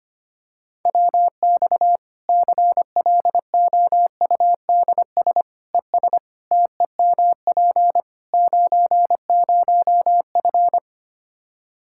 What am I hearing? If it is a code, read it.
Morse code, heard as a keyed tone, about 25 wpm, 700 Hz. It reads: WX CLOUDS ES TEMP 90F
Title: Practice QSO - Exchange 2 - Info - 25wpm